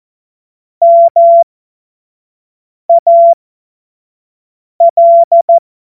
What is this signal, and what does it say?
Morse code, heard as a keyed tone, about 14 wpm, spaced as for 5 wpm, 680 Hz. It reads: MAL
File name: Morsekurs / Witze / Witz 21. März 2023 25 F70